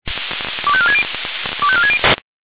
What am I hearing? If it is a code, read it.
Five-tone selective call (ZVEI), heard as quick runs of five tones, tones of about 70 ms. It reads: 25460 25460
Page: SDR/ZVEI